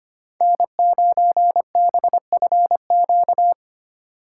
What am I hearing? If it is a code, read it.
Morse code, heard as a keyed tone, about 30 wpm, 700 Hz. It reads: N9BFQ
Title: Speed Racing - USA 1x3 Call Signs 25wpm